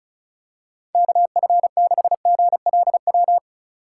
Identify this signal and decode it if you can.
Morse code, heard as a keyed tone, about 36 wpm, 700 Hz. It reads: KF6GLW
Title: USA Call Signs - 4x Character Spacing - 35wpm